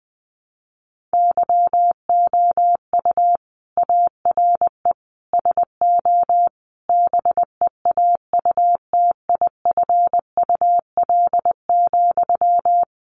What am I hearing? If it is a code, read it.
Morse code, heard as a keyed tone, about 20 wpm, 700 Hz. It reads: YOU ARE SO BEAUTIFUL,
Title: Sentences from Top 1200 Words 20wpm